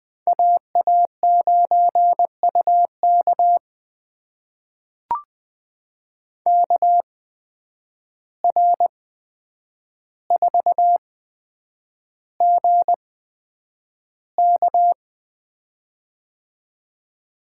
Morse code, heard as a keyed tone, about 20 wpm, spaced as for 9 wpm, 700 Hz. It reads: AA9UK KR4GK
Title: USA Call Signs - 5 Characters - 8x Character Spacing - 20wpm